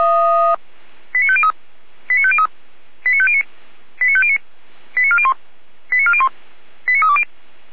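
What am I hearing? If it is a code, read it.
Five-tone selective call (ZVEI), heard as quick runs of five tones, tones of about 70 ms. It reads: 89503 89503 89509 89509 89401 89401 89320
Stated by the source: ZVEI five-tone selective call received by SDR.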